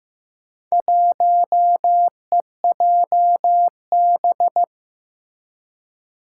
Morse code, heard as a keyed tone, about 15 wpm, 700 Hz. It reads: 1EJB
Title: Four Challenging Letter-Number Combinations 15wpm